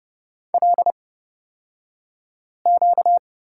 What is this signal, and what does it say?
Morse code, heard as a keyed tone, about 30 wpm, spaced as for 5 wpm, 700 Hz. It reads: LQ
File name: Lesson 38d - Letter q - Two Characters 30wpm